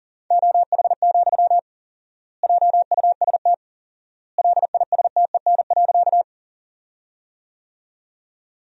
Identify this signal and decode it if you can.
Morse code, heard as a keyed tone, about 40 wpm, 700 Hz. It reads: OH, JUST LISTEN.
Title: 4x Word-Spacing - Sentences from Top 500 Words - 40wpm